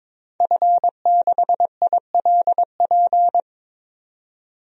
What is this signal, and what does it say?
Morse code, heard as a keyed tone, about 21 wpm, 700 Hz. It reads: F6ILP
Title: DXCC-100 Call Signs - 5 Characters - 22wpm